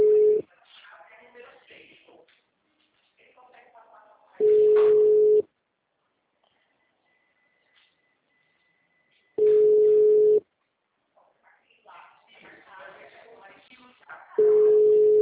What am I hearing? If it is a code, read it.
Ringback tone (Europe, 425 Hz), with speech.
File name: out- -2005- - - .35834